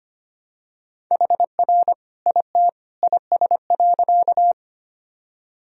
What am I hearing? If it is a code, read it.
Morse code, heard as a keyed tone, about 25 wpm, 700 Hz. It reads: HR IT IS.
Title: Sentences from Top 200 Words and Abbreviations 25wpm